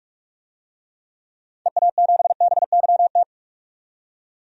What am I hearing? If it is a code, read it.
Morse code, heard as a keyed tone, about 47 wpm, 700 Hz. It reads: EA7BYT
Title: DXCC-100 Call Signs - 6 Characters - 45wpm